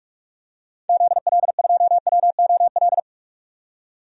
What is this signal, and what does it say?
Morse code, heard as a keyed tone, about 42 wpm, 700 Hz. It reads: ZL2WOL
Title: DXCC-100 Call Signs - 6 Characters - 45wpm